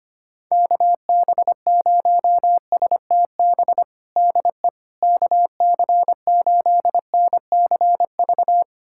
Morse code, heard as a keyed tone, about 25 wpm, 700 Hz. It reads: KB0STB DE KC8NCV
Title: USA Callsign DE Callsign - 6 Characters - 25wpm